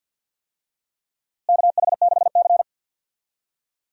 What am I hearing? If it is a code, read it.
Morse code, heard as a keyed tone, about 50 wpm, 700 Hz. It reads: KH6C
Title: DXCC-100 Call Signs - 4 Characters - Repeated 3x - 50wpm